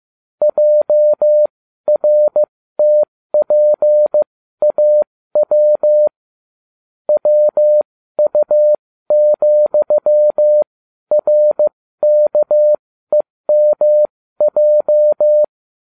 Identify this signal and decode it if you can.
Morse code, heard as a keyed tone, about 15 wpm, spaced as for 12 wpm, 600 Hz. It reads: JRTPAW WU,RKEMJ